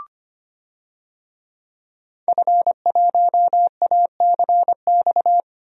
Morse code, heard as a keyed tone, about 25 wpm, 700 Hz. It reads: F1ACX
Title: DXCC-100 Call Signs - 5 Characters - Repeated 3x - 25wpm